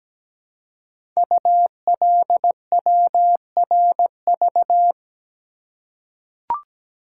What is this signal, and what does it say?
Morse code, heard as a keyed tone, about 17 wpm, 700 Hz. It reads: ULWRV
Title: Code Groups - 5 Letters - 17wpm